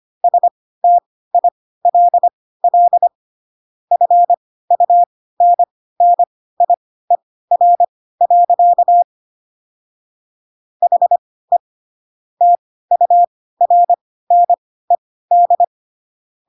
Morse code, heard as a keyed tone, about 25 wpm, spaced as for 16 wpm, 700 Hz. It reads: STILL FUNNIER. HE TURNED
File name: Pinocchio / Pinocchio ch1 10 25 ik1hge